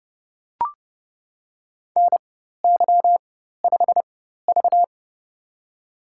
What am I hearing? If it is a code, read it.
Morse code, heard as a keyed tone, about 30 wpm, spaced as for 14 wpm, 700 Hz. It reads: NY5V
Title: USA Call Signs - 4x Character Spacing - 30wpm